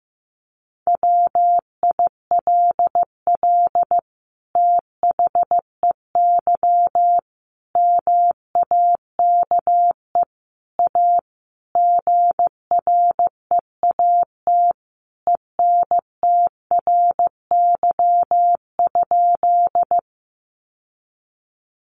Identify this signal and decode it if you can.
Morse code, heard as a keyed tone, about 15 wpm, 700 Hz. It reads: WILL THEY MAKE A GREAT ENTRY?
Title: Sentences from Top 1500 Words 15wpm